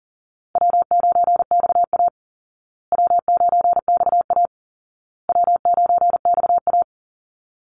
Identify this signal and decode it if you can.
Morse code, heard as a keyed tone, about 40 wpm, 700 Hz. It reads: W9XA W9XA W9XA
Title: Easy Call Signs Repeated 3x - 40wpm